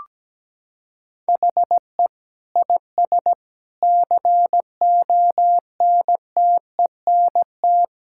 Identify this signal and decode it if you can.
Morse code, heard as a keyed tone, about 17 wpm, 700 Hz. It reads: HE IS CONTENT
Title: Sentences from Top 1000 Words - Review Part 6 - 17wpm